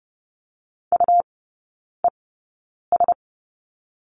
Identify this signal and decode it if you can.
Morse code, heard as a keyed tone, about 30 wpm, spaced as for 9 wpm, 700 Hz. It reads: UES
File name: Lesson 17e - Letter u - Three Characters 30wpm